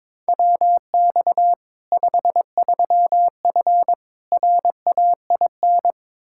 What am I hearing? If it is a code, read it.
Morse code, heard as a keyed tone, about 22 wpm, 700 Hz. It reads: WX 53F RAIN
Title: Practice QSO - Exchange 2 - Info - 22wpm